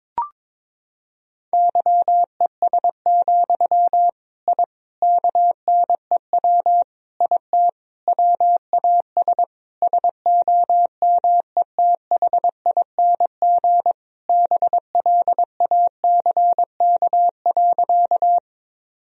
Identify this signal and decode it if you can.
Morse code, heard as a keyed tone, about 22 wpm, 700 Hz. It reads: YES, I KNEW IT WAS SOMETHING BLACK.
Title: Sentences from Top 1000 Words - Review Part 2 - 22wpm